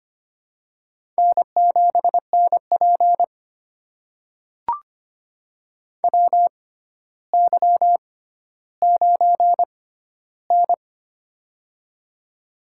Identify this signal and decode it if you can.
Morse code, heard as a keyed tone, about 25 wpm, spaced as for 13 wpm, 700 Hz. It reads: N7NP WY9N
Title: USA Call Signs - 4 Characters - 6x Character Spacing - 25wpm